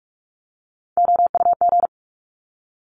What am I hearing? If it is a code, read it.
Morse code, heard as a keyed tone, about 46 wpm, 700 Hz. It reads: OUG
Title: Top Three Letter Combinations in Words 45wpm